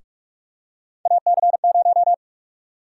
Morse code, heard as a keyed tone, about 43 wpm, 700 Hz. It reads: AC0
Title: USA Call Sign Prefixes - Repeated 2x - 45wpm